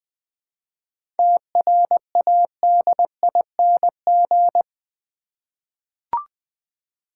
Morse code, heard as a keyed tone, about 20 wpm, 690 Hz. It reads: TRADING
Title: 7 Letter Words - First 3 Letters Given - 20wpm